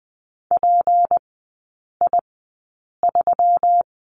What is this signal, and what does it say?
Morse code, heard as a keyed tone, about 20 wpm, spaced as for 8 wpm, 700 Hz. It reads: PI3
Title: 152 Lesson 25e - Letter p - Three Characters 20wpm